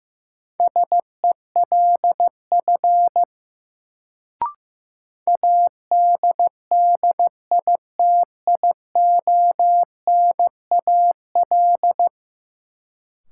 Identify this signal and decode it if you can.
Morse code, heard as a keyed tone, about 15 wpm, 700 Hz. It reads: SELF ADDITIONAL
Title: Sentences from Top 1400 Words - New Word Drill Alt - 15wpm